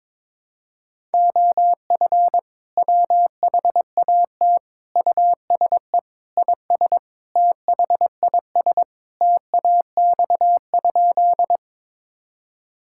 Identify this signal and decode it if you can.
Morse code, heard as a keyed tone, about 22 wpm, 700 Hz. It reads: OF WHAT USE IS THIS TAX?
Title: Sentences from Top 1000 Words - Review Part 5 - 22wpm